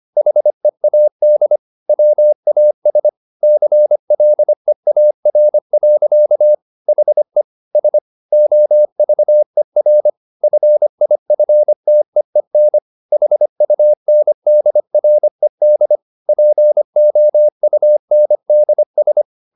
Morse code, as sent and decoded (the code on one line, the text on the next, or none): .... .- -.. / .-- .- ... / -.-. .-.. .- .-. .-.-.- / .... / ... / --- ...- . .-. / ..-. .. ..-. - . -. / .... ..- -. -.. .-. -.. / .--. --- ..- -. -.. ...
HEAD WAS CLEAR. HE S OVER FIFTEEN HUNDRED POUNDS